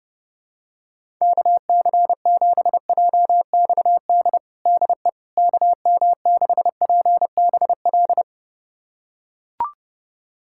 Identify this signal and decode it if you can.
Morse code, heard as a keyed tone, about 30 wpm, 700 Hz. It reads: KC7JXD DE KM6PBL
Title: USA Callsign DE Callsign - 6 Characters - 30wpm